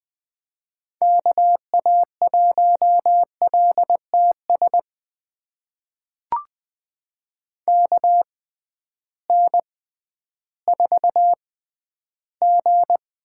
Morse code, heard as a keyed tone, about 20 wpm, spaced as for 12 wpm, 700 Hz. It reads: KA1LTS KN4G
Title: USA Call Signs - 6x Character Spacing - 20wpm